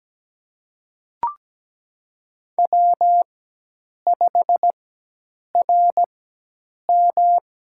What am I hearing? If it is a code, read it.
Morse code, heard as a keyed tone, about 17 wpm, spaced as for 8 wpm, 700 Hz. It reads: W5RM
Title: USA Call Signs - 4 Characters - 4x Character Spacing - 17wpm